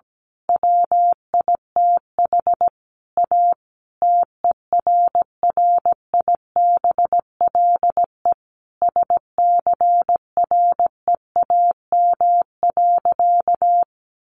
Morse code, heard as a keyed tone, about 17 wpm, 700 Hz. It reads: WITH A TERRIBLE SCREAM.
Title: A Sound Of Thunder 17wpm